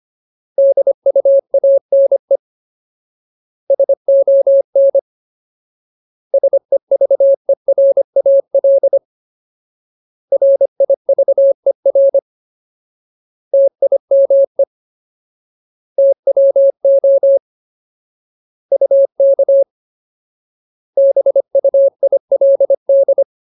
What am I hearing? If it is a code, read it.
Morse code, heard as a keyed tone, about 25 wpm, 550 Hz. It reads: DUANE SON SEVERAL RIVER TIME TWO UK BUILD
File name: Ultimate Word List / 25WPM Ultimate